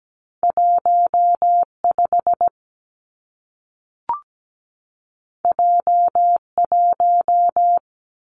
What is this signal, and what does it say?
Morse code, heard as a keyed tone, about 17 wpm, 700 Hz. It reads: 15 J1
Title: Two Challenging Letter-Number Combinations 17wpm